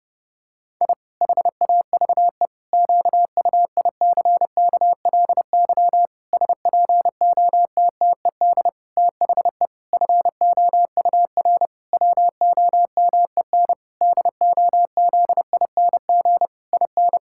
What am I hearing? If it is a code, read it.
Morse code, heard as a keyed tone, about 32 wpm, 700 Hz. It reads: I HAVE QUICKLY SPOTTED THE FOUR WOMEN DOZING IN
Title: Pangram Challenge v2 - 30wpm